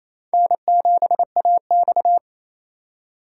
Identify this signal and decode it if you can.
Morse code, heard as a keyed tone, about 35 wpm, 700 Hz. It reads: N7AX
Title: Speed Racing - USA 1x2 Call Signs 28wpm